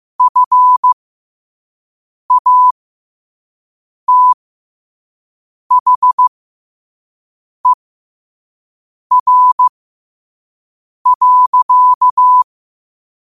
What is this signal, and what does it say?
Morse code, heard as a keyed tone, about 15 wpm, spaced as for 5 wpm, 1000 Hz. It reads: FATHER.
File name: qotdfiles / 5wpm